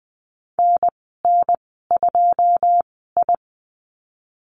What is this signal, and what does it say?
Morse code, heard as a keyed tone, about 20 wpm, spaced as for 15 wpm, 700 Hz. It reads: NN2I
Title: 082 Lesson 14g - Number 2 - Call Signs 20wpm